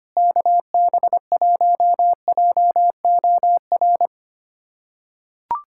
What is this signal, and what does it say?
Morse code, heard as a keyed tone, about 25 wpm, 700 Hz. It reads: KB1JOR
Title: USA Callsign DE Callsign - 6 Characters - 25wpm